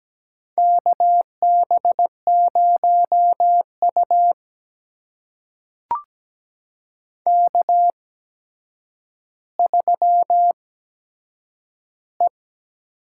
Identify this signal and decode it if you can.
Morse code, heard as a keyed tone, about 17 wpm, spaced as for 8 wpm, 700 Hz. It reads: KB0U K3E
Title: USA Call Signs - 8x Character Spacing - 17wpm